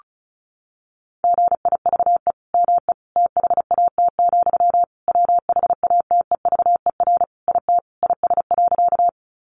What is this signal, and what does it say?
Morse code, heard as a keyed tone, about 34 wpm, 700 Hz. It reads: GIVE ME THAT, WHATEVER IT IS.
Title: Sentences from Top 1100 Words 35wpm